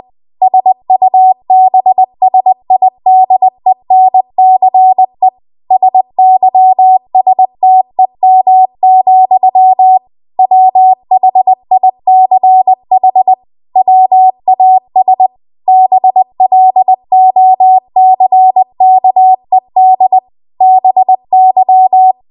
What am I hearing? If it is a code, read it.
Morse code, heard as a keyed tone, about 20 wpm, 750 Hz. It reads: SUBSIDENCE SYSTEM, WHICH WAS BLOCKED BY
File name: Archive / 20 WPM / 20WPM